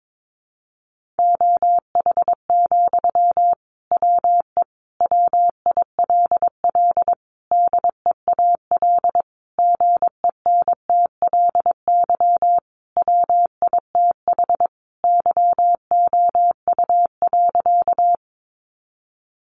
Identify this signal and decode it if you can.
Morse code, heard as a keyed tone, about 22 wpm, 700 Hz. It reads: OH, WE WILL DEAL GENTLY WITH YOU.
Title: Sentences from Top 1500 Words 22wpm